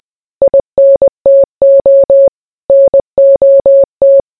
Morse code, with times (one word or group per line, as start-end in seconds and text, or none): 0.4-2.3 INTO
2.7-4.2 NOT